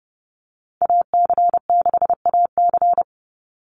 Morse code, heard as a keyed tone, about 30 wpm, 700 Hz. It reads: AC6AC
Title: Call Signs - 30wpm